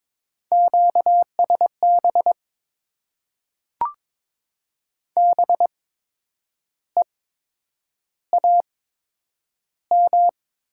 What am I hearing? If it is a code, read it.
Morse code, heard as a keyed tone, about 22 wpm, spaced as for 8 wpm, 700 Hz. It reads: QSB BEAM
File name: Top 90 QSO Elements - 8x Character Spacing - 22wpm